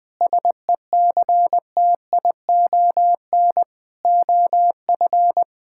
Morse code, heard as a keyed tone, about 20 wpm, 700 Hz. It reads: SECTION OF
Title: Sentences from Top 1000 Words - Review Part 5 - 20wpm